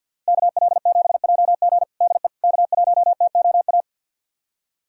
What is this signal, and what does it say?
Morse code, heard as a keyed tone, about 52 wpm, 700 Hz. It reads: KL7JG DE K1TOA
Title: USA Callsign DE Callsign - 5 Characters - 50wpm